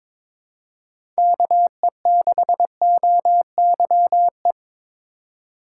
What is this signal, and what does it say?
Morse code, heard as a keyed tone, about 22 wpm, 700 Hz. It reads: KE6OYE
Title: USA Call Signs - 6 Characters - 3x Character Spacing - 22wpm